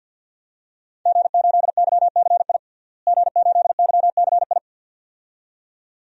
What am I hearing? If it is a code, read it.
Morse code, heard as a keyed tone, about 48 wpm, 700 Hz. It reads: G8YCI G8YCI
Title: DXCC-100 Call Signs - 5 Characters - Repeated 2x - 50wpm